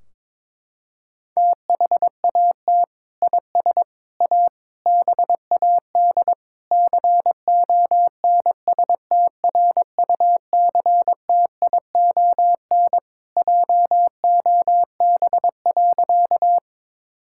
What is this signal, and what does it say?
Morse code, heard as a keyed tone, about 22 wpm, 700 Hz. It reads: THAT IS A BAD CONSTRUCTION JOB.
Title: Sentences from Top 1600 Words 22wpm